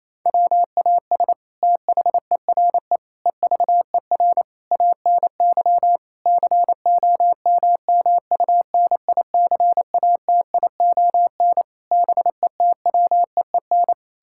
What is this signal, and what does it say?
Morse code, heard as a keyed tone, about 28 wpm, 700 Hz. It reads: WAS THERE EVER ANY COMMUNICATION BETWEEN
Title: Sentences from Top 1000 Words - Review Part 6 - 28wpm